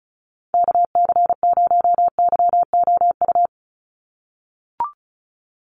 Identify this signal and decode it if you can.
Morse code, heard as a keyed tone, about 36 wpm, 700 Hz. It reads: KC0YOU
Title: USA Call Signs 35wpm